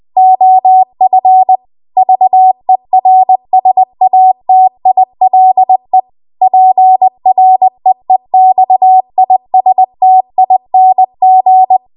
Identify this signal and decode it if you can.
Morse code, heard as a keyed tone, about 20 wpm, 750 Hz. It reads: OF VERSATILE PREEXISTING